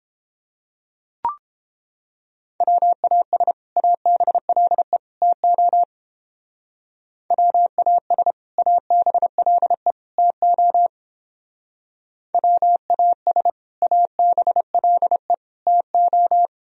Morse code, heard as a keyed tone, about 30 wpm, 700 Hz. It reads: WAS ABLE TO WAS ABLE TO WAS ABLE TO